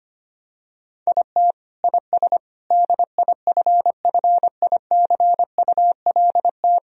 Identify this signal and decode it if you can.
Morse code, heard as a keyed tone, about 25 wpm, 700 Hz. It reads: IT IS DIFFICULT
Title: Sentences from Top 1000 Words - Review Part 4 - 25wpm